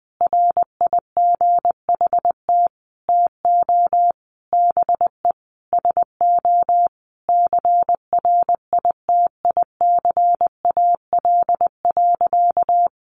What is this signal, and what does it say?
Morse code, heard as a keyed tone, about 20 wpm, 700 Hz. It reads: RIGHT TO BE SO CRITICAL.
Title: Sentences from Top 1100 Words 20wpm